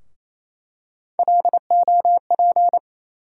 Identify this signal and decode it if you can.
Morse code, heard as a keyed tone, about 27 wpm, 700 Hz. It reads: LOP
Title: USA Call Sign Postfixes - 3 Letters 28wpm